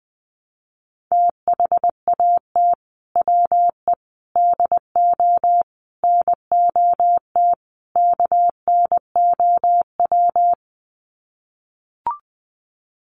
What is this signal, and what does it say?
Morse code, heard as a keyed tone, about 20 wpm, 700 Hz. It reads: THAT WE DO NOT KNOW
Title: Sets of 5 Words Encore2 20wpm